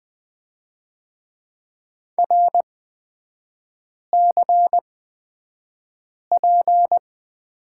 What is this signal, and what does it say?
Morse code, heard as a keyed tone, about 20 wpm, spaced as for 5 wpm, 690 Hz. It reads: RCP
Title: 229 Lesson 37e - Letter x - Three Characters 20wpm